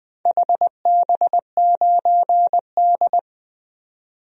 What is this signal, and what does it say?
Morse code, heard as a keyed tone, about 21 wpm, 700 Hz. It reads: HB9D
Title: DXCC-100 Call Signs - 4 Characters - 20wpm